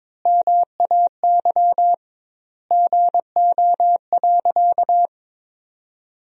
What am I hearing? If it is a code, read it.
Morse code, heard as a keyed tone, about 22 wpm, 700 Hz. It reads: MAY GO.
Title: 2x Word-Spacing - Sentences from Top 100 Words - 22wpm